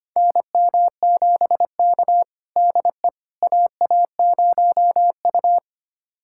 Morse code, heard as a keyed tone, about 25 wpm, 700 Hz. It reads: NM7K DE AA0U
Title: USA Callsign DE Callsign - 4 Characters - 25wpm